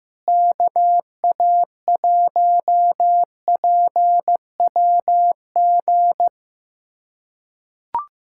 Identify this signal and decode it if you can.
Morse code, heard as a keyed tone, about 15 wpm, 700 Hz. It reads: KA1PWG
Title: USA Callsign DE Callsign 15wpm